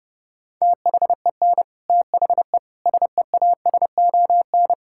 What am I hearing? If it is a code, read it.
Morse code, heard as a keyed tone, about 32 wpm, 700 Hz. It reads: THEN THE SEASON
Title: Sentences from Top 1600 Words 30wpm